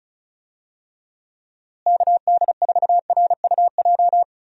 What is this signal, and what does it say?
Morse code, heard as a keyed tone, about 34 wpm, 700 Hz. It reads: KD4RUJ
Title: USA Call Signs - 6 Characters - Repeated 2x - 35wpm